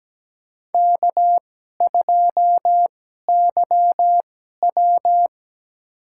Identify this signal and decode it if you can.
Morse code, heard as a keyed tone, about 17 wpm, spaced as for 12 wpm, 700 Hz. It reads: K2YW
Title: USA Call Signs - 4 Characters - 2x Character Spacing - 17wpm